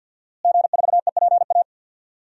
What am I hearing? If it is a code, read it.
Morse code, heard as a keyed tone, about 51 wpm, 700 Hz. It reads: G4EPA
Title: DXCC-100 Call Signs - 5 Characters - Repeated 2x - 50wpm